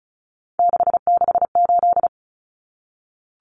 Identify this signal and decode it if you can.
Morse code, heard as a keyed tone, about 34 wpm, 700 Hz. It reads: 668
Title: Common Cut Numbers - 3-Digit 35wpm